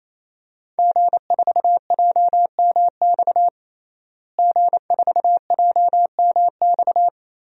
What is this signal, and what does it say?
Morse code, heard as a keyed tone, about 28 wpm, 700 Hz. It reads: G4JMX G4JMX
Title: DXCC-100 Call Signs - 5 Characters - Repeated 3x - 28wpm